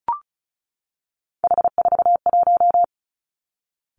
Morse code, heard as a keyed tone, about 36 wpm, 700 Hz. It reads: H41
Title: Three Challenging Letter-Number Combinations 35wpm